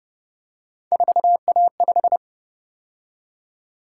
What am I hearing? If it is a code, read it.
Morse code, heard as a keyed tone, about 31 wpm, 700 Hz. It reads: 4A5